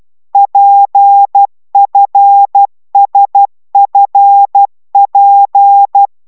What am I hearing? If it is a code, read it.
Morse code, heard as a keyed tone, about 12 wpm, 800 Hz. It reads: PFSFP
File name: taviro tanulas / cw 69 10 10